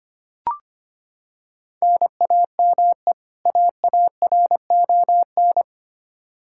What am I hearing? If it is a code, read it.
Morse code, heard as a keyed tone, about 25 wpm, 700 Hz. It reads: NAME AARON
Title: Practice QSO - Exchange 1 - Beginning - 25wpm